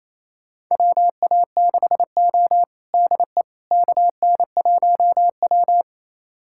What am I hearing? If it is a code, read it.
Morse code, heard as a keyed tone, about 28 wpm, 700 Hz. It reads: WA6O DE KN1W
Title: USA Callsign DE Callsign - 4 Characters - 28wpm